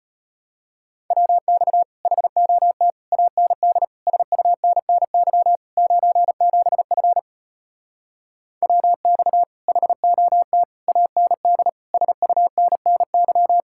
Morse code, heard as a keyed tone, about 33 wpm, 700 Hz. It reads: WX HOT AND SUNNY 97F WX HOT AND SUNNY